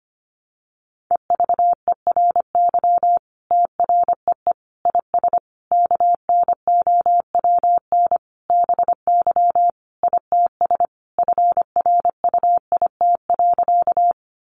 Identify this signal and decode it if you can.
Morse code, heard as a keyed tone, about 25 wpm, 700 Hz. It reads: EVERY TREE IS KNOWN BY ITS FRUIT.
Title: Sentences from Top 1300 Words 25wpm